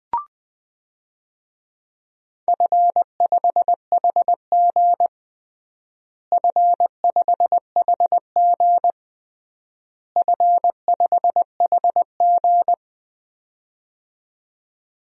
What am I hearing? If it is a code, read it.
Morse code, heard as a keyed tone, about 20 wpm, 700 Hz. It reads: F5HG F5HG F5HG
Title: DXCC-100 Call Signs - 4 Characters - Repeated 3x - 20wpm